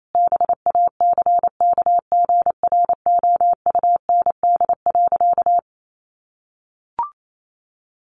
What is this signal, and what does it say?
Morse code, heard as a keyed tone, about 28 wpm, 700 Hz. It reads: BACKGROUND.
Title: Sentences from Top 1300 Words 28wpm